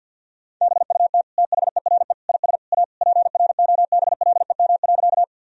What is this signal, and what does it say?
Morse code, heard as a keyed tone, about 46 wpm, 700 Hz. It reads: BUT THERE IS A PROBLEM.
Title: Sentences from Top 1000 Words - Review Part 2 - 50wpm